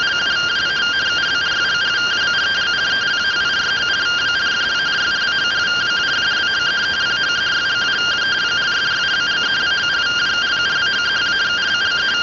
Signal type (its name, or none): two-tone teleprinter signal